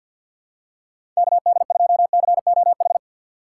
Morse code, heard as a keyed tone, about 53 wpm, 700 Hz. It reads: KD2COH
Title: USA Call Signs - 4x Character Spacing - 50wpm